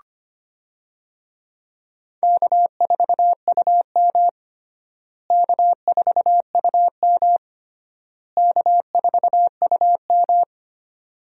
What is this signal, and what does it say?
Morse code, heard as a keyed tone, about 25 wpm, 700 Hz. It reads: K4UM K4UM K4UM
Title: USA Call Signs - 4 Characters - Repeated 3x - 25wpm